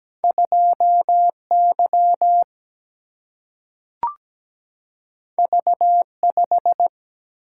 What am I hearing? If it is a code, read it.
Morse code, heard as a keyed tone, about 17 wpm, 700 Hz. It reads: 2Y V5
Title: Single-Letter Single-Number 17wpm